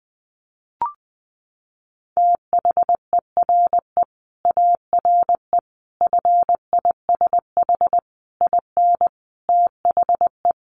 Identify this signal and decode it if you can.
Morse code, heard as a keyed tone, about 20 wpm, 700 Hz. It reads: THERE ARE FISH IN THE
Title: Sentences from Top 1100 Words 20wpm